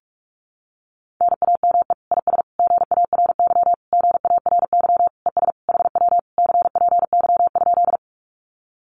Morse code, heard as a keyed tone, about 44 wpm, 700 Hz. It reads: NAME IS GARY GARY ES HW CPY?
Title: Common QSO Phrases Encore 45wpm